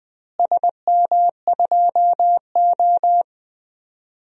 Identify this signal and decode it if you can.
Morse code, heard as a keyed tone, about 20 wpm, 700 Hz. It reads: SM2O
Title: DXCC-100 Call Signs - 4 Characters - Repeated 3x - 20wpm